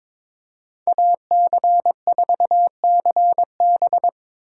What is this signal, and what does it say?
Morse code, heard as a keyed tone, about 26 wpm, 700 Hz. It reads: AC4CB
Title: Speed Racing - USA 2x2 Call Signs 22wpm